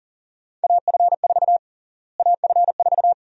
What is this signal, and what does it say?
Morse code, heard as a keyed tone, about 37 wpm, 700 Hz. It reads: AF4 AF4
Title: USA Call Sign Prefixes - Repeated 2x - 40wpm